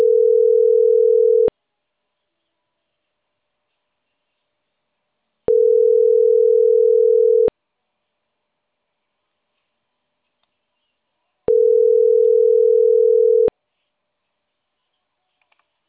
Ringback tone (North America), 440 + 480 Hz.